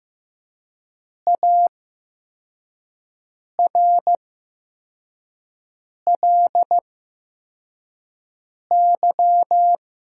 Morse code, heard as a keyed tone, about 15 wpm, spaced as for 4 wpm, 700 Hz. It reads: ARLY
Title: 8x Character Spacing - Top 500 Words - 15wpm